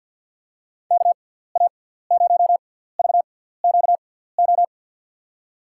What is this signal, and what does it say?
Morse code, heard as a keyed tone, about 53 wpm, spaced as for 17 wpm, 690 Hz. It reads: KA0VQO
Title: USA Call Signs - 6x Character Spacing - 50wpm